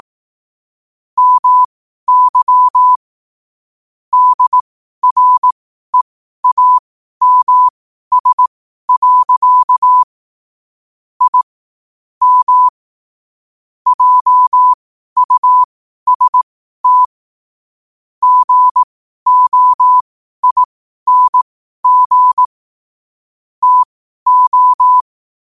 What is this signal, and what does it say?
Morse code, heard as a keyed tone, about 18 wpm, spaced as for 12 wpm, 1000 Hz. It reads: MY DREAMS. I M JUST GOING TO
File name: qotdfiles / 12wpm